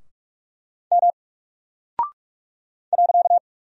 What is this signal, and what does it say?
Morse code, heard as a keyed tone, about 44 wpm, 700 Hz. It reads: M .
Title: Single Character (Letter:Number:Punctuation:Prosign) Alt 45wpm